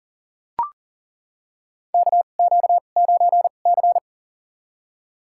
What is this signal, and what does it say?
Morse code, heard as a keyed tone, about 40 wpm, spaced as for 29 wpm, 700 Hz. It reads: KQ9C
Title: USA Call Signs - 2x Character Spacing - 40wpm